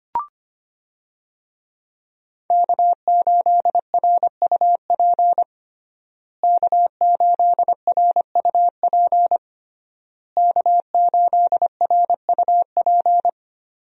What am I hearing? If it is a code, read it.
Morse code, heard as a keyed tone, about 25 wpm, 700 Hz. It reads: K8RUP K8RUP K8RUP
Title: USA Call Signs - 5 Characters - Repeated 3x - 25wpm